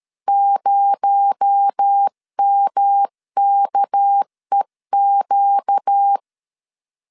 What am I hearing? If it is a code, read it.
Morse code, heard as a keyed tone, about 13 wpm, 800 Hz. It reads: 0MKEQ
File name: LEZIONI / 9 286 12wpm 2305